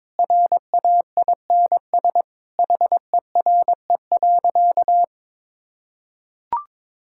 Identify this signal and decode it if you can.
Morse code, heard as a keyed tone, about 22 wpm, 700 Hz. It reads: RAINS HERE.
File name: practice / 22wpm / Easy Sentences with Hints - 22wpm